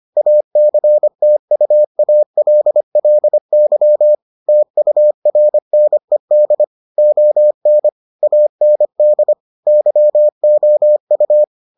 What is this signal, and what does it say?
Morse code, heard as a keyed tone, about 25 wpm, 600 Hz. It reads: ACTUALLY TURNED ON AND YOU